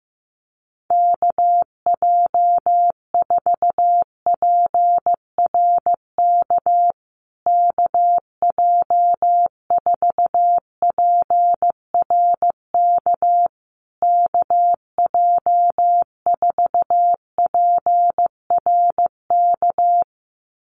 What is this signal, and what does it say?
Morse code, heard as a keyed tone, about 15 wpm, 700 Hz. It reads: KJ4PRK KJ4PRK KJ4PRK
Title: USA Call Signs Repeated 3x 15wpm